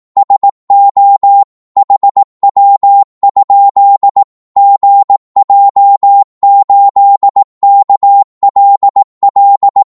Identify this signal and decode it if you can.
Morse code, heard as a keyed tone, about 18 wpm, 800 Hz. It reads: SO HW? GJ8KLL
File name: Morse / qso-18wpm